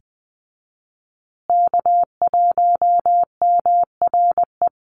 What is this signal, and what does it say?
Morse code, heard as a keyed tone, about 20 wpm, 700 Hz. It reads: K1MRE
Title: USA Call Signs Repeated 2x 20wpm